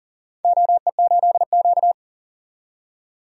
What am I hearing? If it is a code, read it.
Morse code, heard as a keyed tone, about 40 wpm, 700 Hz. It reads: OE8Q
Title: DXCC-100 Call Signs - 4 Characters - 40wpm